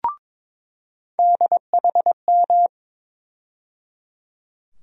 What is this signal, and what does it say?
Morse code, heard as a keyed tone, about 22 wpm, 700 Hz. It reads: DHM
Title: USA Call Sign Postfixes - 3 Letters 22wpm